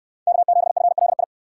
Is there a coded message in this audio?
Yes: Morse code from 0.3 to 1.2 s, heard as a keyed tone, about 68 wpm, 700 Hz. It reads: K7FLI